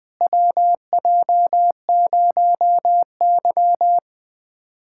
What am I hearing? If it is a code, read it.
Morse code, heard as a keyed tone, about 20 wpm, 700 Hz. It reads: WJ0Y
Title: USA Callsign DE Callsign - 4 Characters - 20wpm